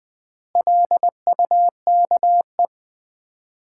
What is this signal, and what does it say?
Morse code, heard as a keyed tone, about 20 wpm, 700 Hz. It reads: LUKE
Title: Common QSO Phrases - 4x Word-Spacing - 20wpm